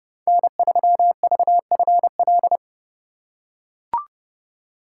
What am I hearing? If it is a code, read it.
Morse code, heard as a keyed tone, about 30 wpm, 700 Hz. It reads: N3VFL
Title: USA Call Signs - 5 Characters - 4x Character Spacing - 30wpm